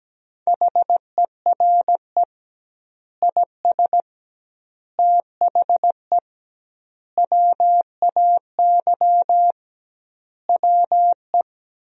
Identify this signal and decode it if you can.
Morse code, heard as a keyed tone, about 17 wpm, 700 Hz. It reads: HERE IS THE WAY WE
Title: 2x Word-Spacing - Sentences from Top 200 Words - 17wpm